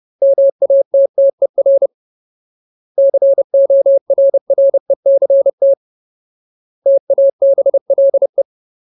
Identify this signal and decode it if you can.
Morse code, heard as a keyed tone, about 30 wpm, 550 Hz. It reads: MATTER CORRECT TABLE
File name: Ultimate Word List / 30WPM Ultimate